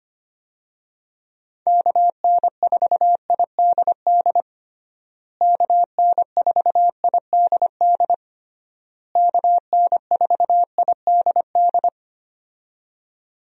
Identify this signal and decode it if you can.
Morse code, heard as a keyed tone, about 25 wpm, 700 Hz. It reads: KN4IDD KN4IDD KN4IDD
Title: USA Call Signs - 6 Characters - Repeated 3x - 25wpm